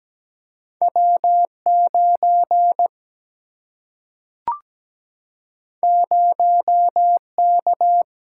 Morse code, heard as a keyed tone, about 17 wpm, 700 Hz. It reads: W 9 0 K
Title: Single-Letter Single-Number 17wpm